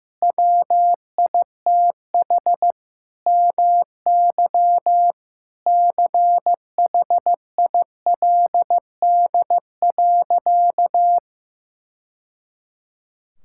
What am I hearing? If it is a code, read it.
Morse code, heard as a keyed tone, about 15 wpm, 700 Hz. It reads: WITH MY CHILD.
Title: Sentences from Top 1000 Words - Review Part 6 - 15wpm